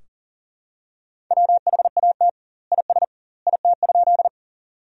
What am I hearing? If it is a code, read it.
Morse code, heard as a keyed tone, about 40 wpm, 700 Hz. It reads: WHAT IS IT?
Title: 2x Word-Spacing - Sentences from Top 100 Words - 40wpm